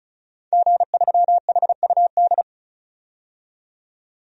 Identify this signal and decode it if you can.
Morse code, heard as a keyed tone, about 36 wpm, 700 Hz. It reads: G3HUD
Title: DXCC-100 Call Signs - 5 Characters - 35wpm